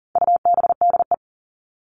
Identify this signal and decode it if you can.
Morse code, heard as a keyed tone, about 40 wpm, 700 Hz. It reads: UBDE
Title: All Cut Numbers - 4-Digit 40wpm